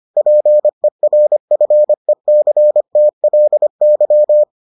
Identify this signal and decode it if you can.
Morse code, heard as a keyed tone, about 25 wpm, 600 Hz. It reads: PERFECTLY